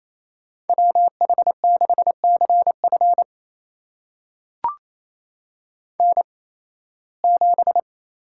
Morse code, heard as a keyed tone, about 28 wpm, spaced as for 18 wpm, 700 Hz. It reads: WH6CF N7
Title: USA Call Signs - 8x Character Spacing - 28wpm